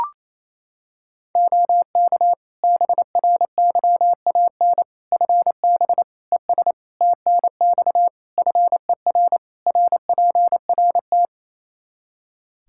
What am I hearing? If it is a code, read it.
Morse code, heard as a keyed tone, about 28 wpm, 700 Hz. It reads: OK BRYAN FB ES TNX FER RPRT
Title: Practice QSO - Exchange 2 - Info - 28wpm